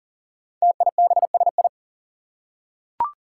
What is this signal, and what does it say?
Morse code, heard as a keyed tone, about 40 wpm, 700 Hz. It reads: TIBSI
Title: Five Challenging Letter-Number Combinations 40wpm